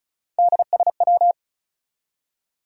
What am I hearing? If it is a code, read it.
Morse code, heard as a keyed tone, about 48 wpm, spaced as for 42 wpm, 700 Hz. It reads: DSW
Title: Speed-Racing Abbreviations - 35wpm